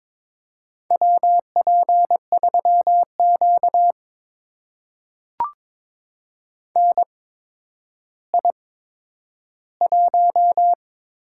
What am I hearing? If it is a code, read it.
Morse code, heard as a keyed tone, about 22 wpm, spaced as for 11 wpm, 700 Hz. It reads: WP3Q NI1
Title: USA Call Signs - 4 Characters - 8x Character Spacing - 22wpm